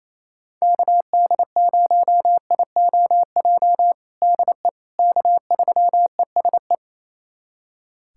Morse code, heard as a keyed tone, about 28 wpm, 700 Hz. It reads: KD0IOJ DE K3ESE
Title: USA Callsign DE Callsign 28wpm